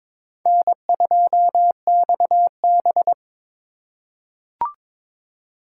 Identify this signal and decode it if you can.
Morse code, heard as a keyed tone, about 22 wpm, 700 Hz. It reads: N2XB
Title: USA Call Signs - 4 Characters - 4x Character Spacing - 22wpm